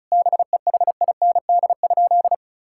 Morse code, heard as a keyed tone, about 34 wpm, 700 Hz. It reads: BEHIND?
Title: Sentences from Top 1000 Words - Review Part 5 - 35wpm